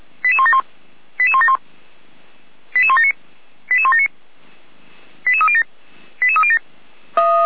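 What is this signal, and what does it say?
Five-tone selective call (ZVEI), heard as quick runs of five tones, tones of about 70 ms. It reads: 88171 88171 88179 88179 88307 88307
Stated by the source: SDR-received ZVEI five-tone selective call, code 88391.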